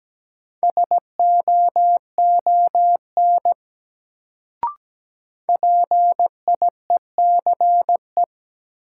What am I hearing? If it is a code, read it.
Morse code, heard as a keyed tone, about 17 wpm, 700 Hz. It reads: SOON PIECE
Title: Sentences from Top 400 Words - New Word Drill - 17wpm